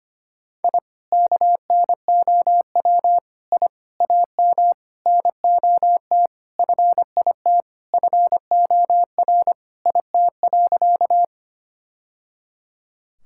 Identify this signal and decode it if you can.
Morse code, heard as a keyed tone, about 25 wpm, 700 Hz. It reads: I KNOW I AM NOT FIT FOR IT.
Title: Sentences from Top 1000 Words - Review Part 5 - 25wpm